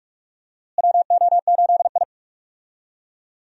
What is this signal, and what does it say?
Morse code, heard as a keyed tone, about 44 wpm, 700 Hz. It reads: WO8I
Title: USA Call Signs - 4 Characters - Repeated 3x - 45wpm